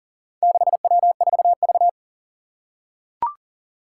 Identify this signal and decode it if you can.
Morse code, heard as a keyed tone, about 40 wpm, 700 Hz. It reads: 6W4V
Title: Four Challenging Letter-Number Combinations 40wpm